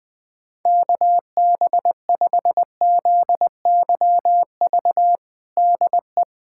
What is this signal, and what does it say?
Morse code, heard as a keyed tone, about 20 wpm, 700 Hz. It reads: KB5ZYV DE
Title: USA Callsign DE Callsign - 6 Characters - 20wpm